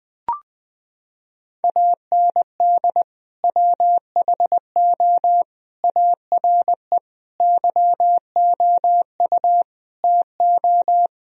Morse code, heard as a keyed tone, about 20 wpm, 700 Hz. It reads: AND WHO ARE YOU TO
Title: Sentences from Top 1000 Words - Review Part 2 - 20wpm